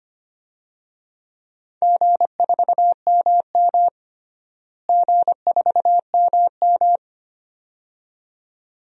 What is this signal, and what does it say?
Morse code, heard as a keyed tone, about 25 wpm, 700 Hz. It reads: G4MM G4MM
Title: DXCC-100 Call Signs - 4 Characters - Repeated 2x - 25wpm